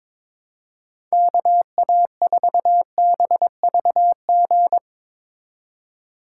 Morse code, heard as a keyed tone, about 21 wpm, 700 Hz. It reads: KA4BVG
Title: K1USN SST QSO Practice - 22wpm